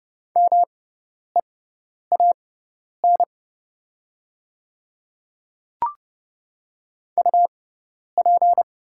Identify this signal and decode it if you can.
Morse code, heard as a keyed tone, about 30 wpm, spaced as for 10 wpm, 700 Hz. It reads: MEAN UP
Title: 6x Character Spacing - Top 200 Words - 30wpm